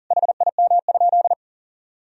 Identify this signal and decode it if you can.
Morse code, heard as a keyed tone, about 37 wpm, 700 Hz. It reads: HIM?
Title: Sentences from Top 1000 Words - Review Part 4 - 40wpm